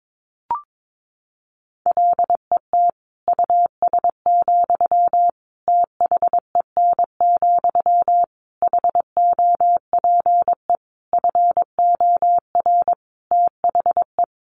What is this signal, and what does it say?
Morse code, heard as a keyed tone, about 22 wpm, 700 Hz. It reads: LET US, THEN, HOPE FOR THE
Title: Sentences from Top 1000 Words 22wpm